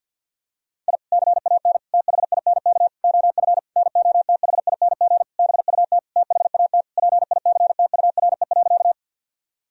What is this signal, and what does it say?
Morse code, heard as a keyed tone, about 51 wpm, 700 Hz. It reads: I CAN THINK OF NOTHING BUT THAT PICTURE.
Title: Sentences from Top 1000 Words - Review Part 1 - 50wpm